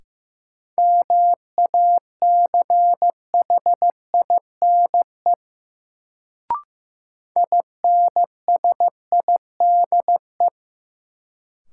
Morse code, heard as a keyed tone, about 15 wpm, 700 Hz. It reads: MACHINE INSIDE
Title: Sentences from Top 1000 Words - Review Part 2 - New Word Drill - 15wpm